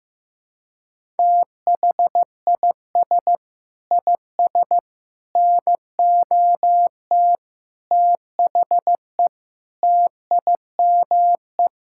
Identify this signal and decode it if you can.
Morse code, heard as a keyed tone, about 15 wpm, 700 Hz. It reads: THIS IS NOT THE TIME
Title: Sentences from Top 1000 Words - Review Part 5 - 15wpm